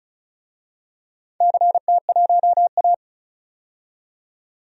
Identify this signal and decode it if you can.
Morse code, heard as a keyed tone, about 36 wpm, 700 Hz. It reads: CT1A
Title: DXCC-100 Call Signs - 4 Characters - 35wpm